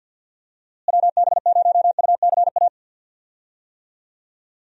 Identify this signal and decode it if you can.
Morse code, heard as a keyed tone, about 52 wpm, 700 Hz. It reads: WB0UCA
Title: USA Call Signs - 6 Characters - Repeated 2x - 50wpm